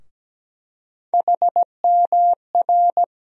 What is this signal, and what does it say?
Morse code, heard as a keyed tone, about 17 wpm, 700 Hz. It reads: HMR
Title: USA Call Sign Postfixes - 3 Letters 17wpm